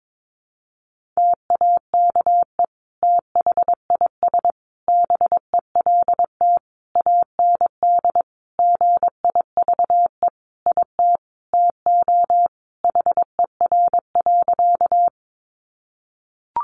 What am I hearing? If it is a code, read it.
Morse code, heard as a keyed tone, about 22 wpm, 700 Hz. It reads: TAKE THIS BELT AND GIVE IT TO HER.
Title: Sentences from Top 1400 Words 22wpm